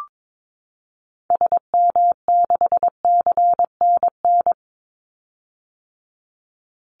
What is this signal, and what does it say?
Morse code, heard as a keyed tone, about 22 wpm, 700 Hz. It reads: SM6CNN
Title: Call Signs - 22wpm